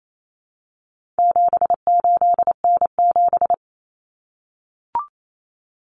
Morse code, heard as a keyed tone, about 28 wpm, 700 Hz. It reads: 78N7
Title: Common Cut Numbers - 4-Digit 28wpm